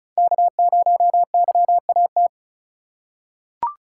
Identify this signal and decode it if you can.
Morse code, heard as a keyed tone, about 36 wpm, 690 Hz. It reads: K0YAT
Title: USA Call Signs - 3x Character Spacing - 35wpm